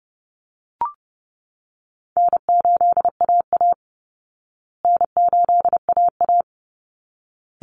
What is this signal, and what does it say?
Morse code, heard as a keyed tone, about 30 wpm, 700 Hz. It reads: N8AA N8AA
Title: Easy Call Signs Repeated 2x - 30wpm